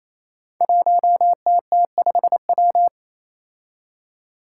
Morse code, heard as a keyed tone, about 28 wpm, 700 Hz. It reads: 1TT5W
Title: Five Challenging Letter-Number Combinations 28wpm